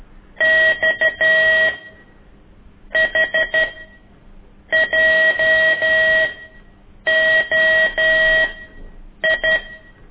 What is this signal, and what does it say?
Morse code, heard as a keyed tone, about 13 wpm, spaced as for 7 wpm, 1900 Hz, with noise. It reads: X HJOI
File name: ik1weg2 / CORSI CW